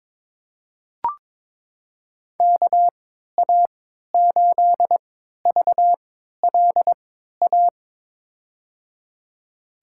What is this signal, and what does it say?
Morse code, heard as a keyed tone, about 22 wpm, spaced as for 12 wpm, 700 Hz. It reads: KA8VLA
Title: USA Call Signs - 6 Characters - 3x Character Spacing - 22wpm